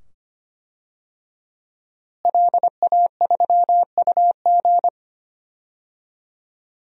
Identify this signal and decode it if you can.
Morse code, heard as a keyed tone, about 25 wpm, 700 Hz. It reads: LA3UG
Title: DXCC-100 Call Signs - 5 Characters - Repeated 2x - 25wpm